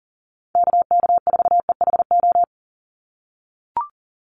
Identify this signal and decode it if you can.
Morse code, heard as a keyed tone, about 40 wpm, 700 Hz. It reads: KK4EHO
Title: USA Call Signs 40wpm